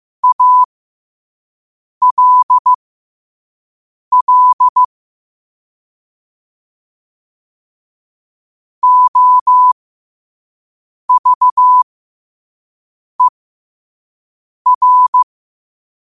Morse code, heard as a keyed tone, about 15 wpm, spaced as for 5 wpm, 1000 Hz. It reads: ALL OVER
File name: qotdfiles / 5wpm